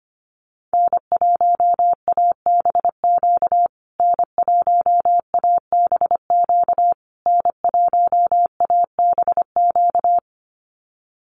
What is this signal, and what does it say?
Morse code, heard as a keyed tone, about 25 wpm, 700 Hz. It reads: N1ABQ N1ABQ N1ABQ
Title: USA Call Signs Repeated 3x 25wpm